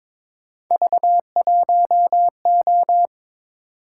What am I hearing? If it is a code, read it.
Morse code, heard as a keyed tone, about 26 wpm, 700 Hz. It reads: V1O
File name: Speed-Racing - Three Letter-Number Combinations - 22wpm